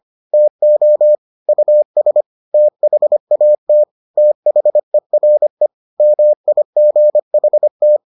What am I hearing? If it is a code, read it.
Morse code, heard as a keyed tone, about 25 wpm, 600 Hz. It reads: TO US THAT THERE MIGHT